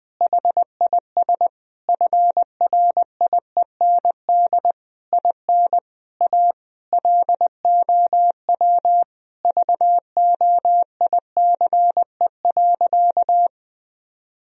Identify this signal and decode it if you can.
Morse code, heard as a keyed tone, about 21 wpm, 700 Hz. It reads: HIS FRIEND IN A LOW VOICE.
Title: Sentences from Top 1000 Words - Review Part 1 - 20wpm